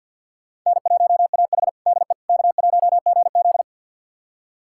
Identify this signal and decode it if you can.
Morse code, heard as a keyed tone, about 48 wpm, 700 Hz. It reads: N1AH DE K1GZ
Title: USA Callsign DE Callsign - 4 Characters - 50wpm